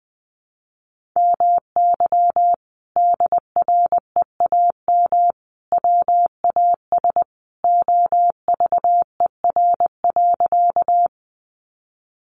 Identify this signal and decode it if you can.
Morse code, heard as a keyed tone, about 20 wpm, 700 Hz. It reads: MY DREAM WAS OVER.
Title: Sentences from Top 1500 Words 20wpm